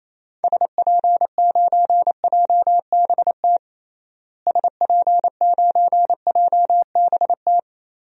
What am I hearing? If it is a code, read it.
Morse code, heard as a keyed tone, about 28 wpm, 700 Hz. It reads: SP9JBT SP9JBT
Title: DXCC-100 Call Signs - 6 Characters - Repeated 3x - 28wpm